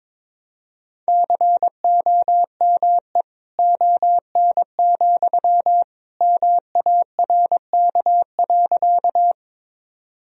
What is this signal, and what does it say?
Morse code, heard as a keyed tone, about 22 wpm, 700 Hz. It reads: COME ON, MARK.
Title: Sentences from Top 1000 Words - Review Part 1 - 22wpm